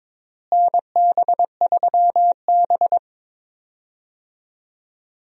Morse code, heard as a keyed tone, about 22 wpm, 700 Hz. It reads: NB3B
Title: USA Call Signs - 4 Characters - Repeated 3x - 22wpm